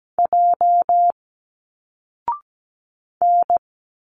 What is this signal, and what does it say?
Morse code, heard as a keyed tone, about 17 wpm, 700 Hz. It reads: J N
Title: Single Letters 17wpm